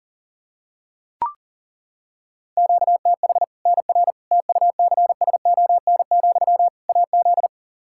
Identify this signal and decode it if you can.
Morse code, heard as a keyed tone, about 40 wpm, 700 Hz. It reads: QTH NR TUCSON, AZ
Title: K1USN SST QSO Practice 40wpm